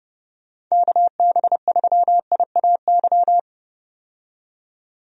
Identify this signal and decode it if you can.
Morse code, heard as a keyed tone, about 31 wpm, 700 Hz. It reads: KB3IAY
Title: USA Call Signs - 6 Characters - 30wpm